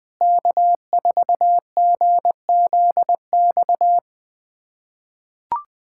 Morse code, heard as a keyed tone, about 20 wpm, 700 Hz. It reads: K4GZX
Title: USA Callsign DE Callsign - 5 Characters - 20wpm